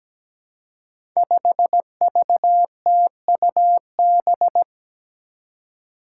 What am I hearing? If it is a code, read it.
Morse code, heard as a keyed tone, about 17 wpm, 700 Hz. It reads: 5VTUB
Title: Five Challenging Letter-Number Combinations 17wpm